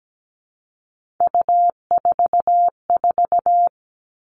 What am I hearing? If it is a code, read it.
Morse code, heard as a keyed tone, about 17 wpm, 700 Hz. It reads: U44
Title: All Cut Numbers - 3-Digit 17wpm